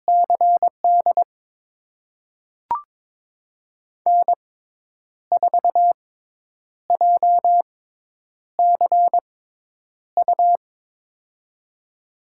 Morse code, heard as a keyed tone, about 22 wpm, spaced as for 9 wpm, 700 Hz. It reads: CD N4JCU
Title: USA Call Signs - 5 Characters - 6x Character Spacing - 22wpm